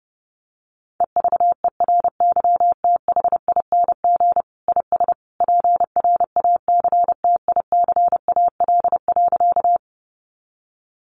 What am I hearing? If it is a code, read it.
Morse code, heard as a keyed tone, about 30 wpm, 700 Hz. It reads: EVERYTHING IS PRACTICAL.
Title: Sentences from Top 1400 Words 30wpm